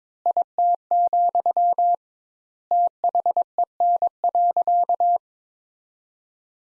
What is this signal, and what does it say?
Morse code, heard as a keyed tone, about 22 wpm, 700 Hz. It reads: IT, THEN.
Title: Sentences Top 200 Words - 22wpm